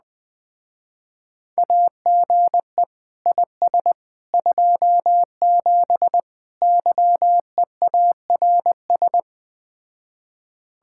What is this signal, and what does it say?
Morse code, heard as a keyed tone, about 20 wpm, 700 Hz. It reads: AGE IS 27 YEARS
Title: Speed-Racing Common QSO Phrases Encore - 17wpm